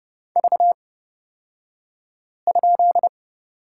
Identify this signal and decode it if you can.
Morse code, heard as a keyed tone, about 30 wpm, spaced as for 5 wpm, 700 Hz. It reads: V?
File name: Lesson 37d - Letter x - Two Characters 30wpm